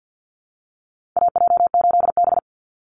Morse code, heard as a keyed tone, about 50 wpm, 700 Hz. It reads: AJ8B
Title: Call Signs - 50wpm